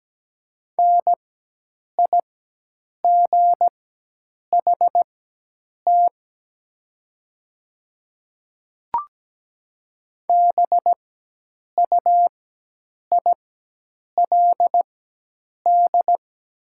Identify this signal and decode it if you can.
Morse code, heard as a keyed tone, about 17 wpm, spaced as for 8 wpm, 700 Hz. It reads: NIGHT BUILD
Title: 4x Character Spacing - Top 500 Words - 17wpm